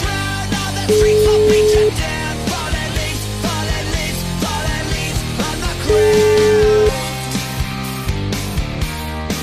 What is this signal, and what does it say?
Ringback tone (Europe, 425 Hz), with noise.